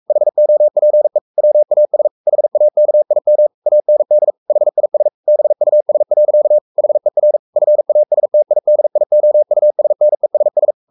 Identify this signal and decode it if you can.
Morse code, heard as a keyed tone, about 44 wpm, 600 Hz. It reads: HOPE WAS HAKIM AND HIS BUS. HER FASTIDIOUSNESS
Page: Romanen Never af Ken Follet på engelsk.